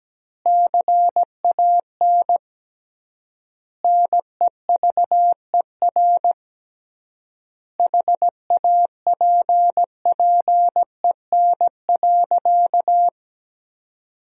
Morse code, heard as a keyed tone, about 17 wpm, 700 Hz. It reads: CAN NEVER HAPPEN.
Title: 3x Word-Spacing - Sentences from Top 400 Words - 17wpm